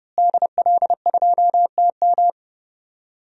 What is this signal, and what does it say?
Morse code, heard as a keyed tone, about 30 wpm, 700 Hz. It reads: DL2TM
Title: DXCC-100 Call Signs - 5 Characters - Repeated 3x - 30wpm